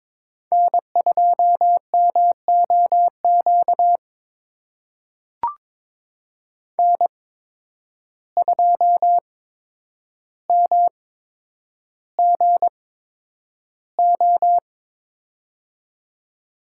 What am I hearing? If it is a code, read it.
Morse code, heard as a keyed tone, about 22 wpm, spaced as for 9 wpm, 700 Hz. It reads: N2MOQ N2MGO
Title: USA Call Signs - 5 Characters - 8x Character Spacing - 22wpm